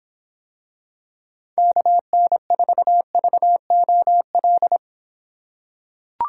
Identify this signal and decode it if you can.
Morse code, heard as a keyed tone, about 26 wpm, 700 Hz. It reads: KN4VOL